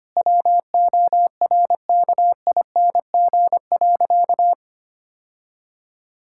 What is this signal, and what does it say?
Morse code, heard as a keyed tone, about 25 wpm, 700 Hz. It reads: WORKING.
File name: practice / 25wpm / Sentences with 3-Word Phrases - 25wpm